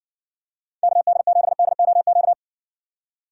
Morse code, heard as a keyed tone, about 62 wpm, 700 Hz. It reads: KD7DOQ